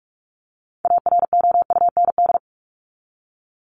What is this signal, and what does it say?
Morse code, heard as a keyed tone, about 44 wpm, 700 Hz. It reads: AROUND
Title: Top 501-750 Words 45wpm